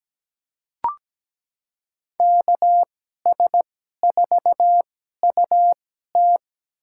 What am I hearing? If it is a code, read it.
Morse code, heard as a keyed tone, about 17 wpm, spaced as for 12 wpm, 700 Hz. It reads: KS4UT
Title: USA Call Signs - 2x Character Spacing - 17wpm